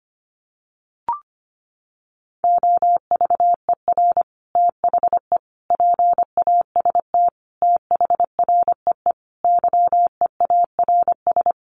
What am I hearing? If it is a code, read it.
Morse code, heard as a keyed tone, about 25 wpm, 700 Hz. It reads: OVER THE PAST THREE YEARS
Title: Sets of 5 Words Encore2 25wpm